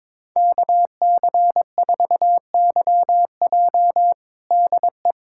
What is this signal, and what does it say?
Morse code, heard as a keyed tone, about 22 wpm, 700 Hz. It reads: KC4YJ DE
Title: USA Callsign DE Callsign - 5 Characters - 22wpm